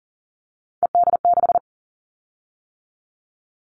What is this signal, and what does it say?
Morse code, heard as a keyed tone, about 38 wpm, 700 Hz. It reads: ED6
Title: Callsign Prefixes for Top 50 DXCC - 40wpm